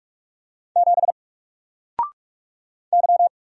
Morse code, heard as a keyed tone, about 46 wpm, 700 Hz. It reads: 7 Y
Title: Single Letter-Number 45wpm